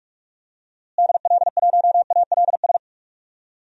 Morse code, heard as a keyed tone, about 46 wpm, 700 Hz. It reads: DL1ALS
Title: DXCC-100 Call Signs - 6 Characters - Repeated 3x - 45wpm